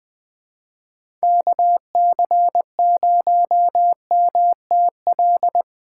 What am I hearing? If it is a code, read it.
Morse code, heard as a keyed tone, about 20 wpm, 700 Hz. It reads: KC0MTL
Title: USA Call Signs - 6 Characters - 2x Character Spacing - 20wpm